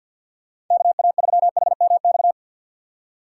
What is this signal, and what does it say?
Morse code, heard as a keyed tone, about 53 wpm, 700 Hz. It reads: KA3HMX
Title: USA Call Signs - 6 Characters - 6x Character Spacing - 50wpm